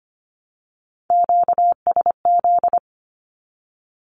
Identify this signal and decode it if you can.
Morse code, heard as a keyed tone, about 25 wpm, 700 Hz. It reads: QSZ
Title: Q-Codes Repeated 3x 25wpm